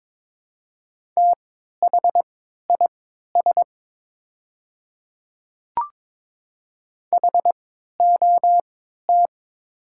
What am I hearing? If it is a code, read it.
Morse code, heard as a keyed tone, about 22 wpm, spaced as for 12 wpm, 700 Hz. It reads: THIS HOT
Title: 3x Character Spacing - Top 100 Words - 22wpm